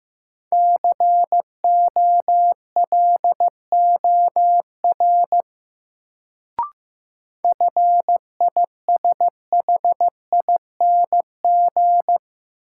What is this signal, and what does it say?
Morse code, heard as a keyed tone, about 16 wpm, 700 Hz. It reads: COLOR FISHING
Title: Sentences from Top 1600 Words - New Word Drill Alt - 15wpm